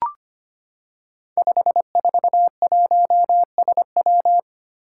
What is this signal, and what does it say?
Morse code, heard as a keyed tone, about 25 wpm, 700 Hz. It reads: 541SW
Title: Five Challenging Letter-Number Combinations 25wpm